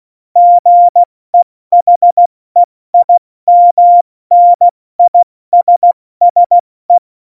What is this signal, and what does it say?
Morse code, heard as a keyed tone, about 16 wpm, 700 Hz. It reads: GEHEIMNISSE